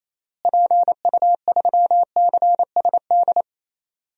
Morse code, heard as a keyed tone, about 28 wpm, 700 Hz. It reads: PU3CSD
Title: DXCC-100 Call Signs - 6 Characters - Repeated 2x - 28wpm